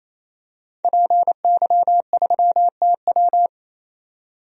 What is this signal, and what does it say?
Morse code, heard as a keyed tone, about 28 wpm, 700 Hz. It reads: PY3TW
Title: DXCC-100 Call Signs - 5 Characters - 28wpm